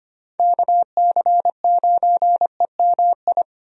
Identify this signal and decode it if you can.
Morse code, heard as a keyed tone, about 25 wpm, 700 Hz. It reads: KC9EMI
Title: USA Call Signs - 6 Characters - Repeated 2x - 25wpm